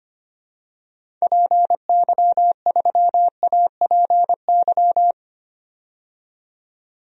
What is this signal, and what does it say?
Morse code, heard as a keyed tone, about 25 wpm, 700 Hz. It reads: PY3APY
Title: DXCC-100 Call Signs - 6 Characters - 25wpm